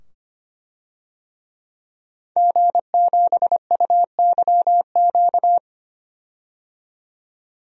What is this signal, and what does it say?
Morse code, heard as a keyed tone, about 25 wpm, 700 Hz. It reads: G7UYQ
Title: DXCC-100 Call Signs - 5 Characters - 25wpm